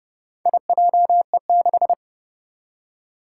Morse code, heard as a keyed tone, about 30 wpm, 700 Hz. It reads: IJE6
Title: Four Challenging Letter-Number Combinations 30wpm